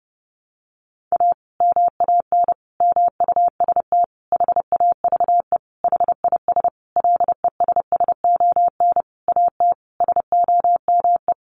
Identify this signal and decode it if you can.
Morse code, heard as a keyed tone, about 30 wpm, 700 Hz. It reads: A MAN MUST HAVE HIS LESSON AT SOME
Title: Sentences from Top 1200 Words 30wpm